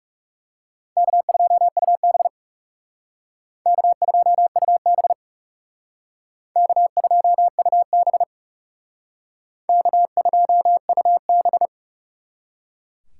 Morse code, heard as a keyed tone, about 36 wpm, 700 Hz. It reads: K2UB K2UB K2UB K2UB
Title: Speed Racing - USA 1x2 Call Signs 30wpm